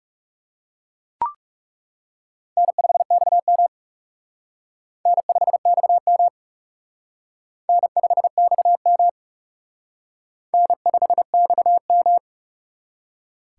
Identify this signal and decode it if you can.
Morse code, heard as a keyed tone, about 35 wpm, 700 Hz. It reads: N5XM N5XM N5XM N5XM
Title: Speed Racing - USA 1x2 Call Signs 30wpm